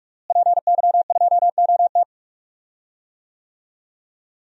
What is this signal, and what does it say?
Morse code, heard as a keyed tone, about 43 wpm, 700 Hz. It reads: PY2OT
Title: DXCC-100 Call Signs - 5 Characters - Repeated 3x - 45wpm